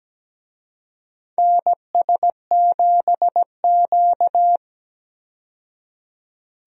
Morse code, heard as a keyed tone, about 17 wpm, 700 Hz. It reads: NS7Q
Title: USA Call Signs - 4 Characters - 17wpm